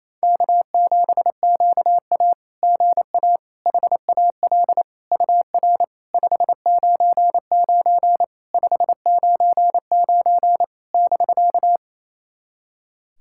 Morse code, heard as a keyed tone, about 28 wpm, 700 Hz. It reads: K7QA GA HAL UR 599 599 <BK>
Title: SOTA QSO Practice 28wpm